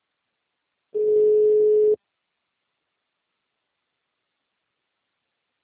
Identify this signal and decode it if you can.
Ringback tone (Europe, 425 Hz).